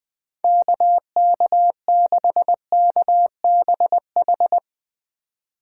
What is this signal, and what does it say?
Morse code, heard as a keyed tone, about 22 wpm, 700 Hz. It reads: KK6KBH
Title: Speed Racing - USA 2x3 Call Signs 20wpm